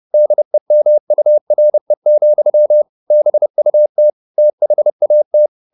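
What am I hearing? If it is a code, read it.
Morse code, heard as a keyed tone, about 30 wpm, 600 Hz. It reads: DEMURE, BUT THAT
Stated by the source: Morse code at 30 wpm, the novel Never by Ken Follett in English, from CW med Gnister.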